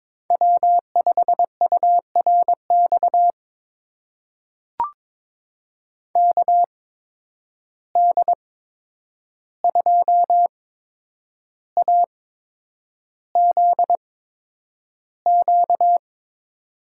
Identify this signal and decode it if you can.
Morse code, heard as a keyed tone, about 22 wpm, spaced as for 9 wpm, 700 Hz. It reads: W5URX KD2AZQ
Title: USA Call Signs - 8x Character Spacing - 22wpm